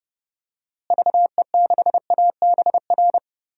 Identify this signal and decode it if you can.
Morse code, heard as a keyed tone, about 30 wpm, 700 Hz. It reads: VE6ABR
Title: DXCC-100 Call Signs - 6 Characters - Repeated 2x - 30wpm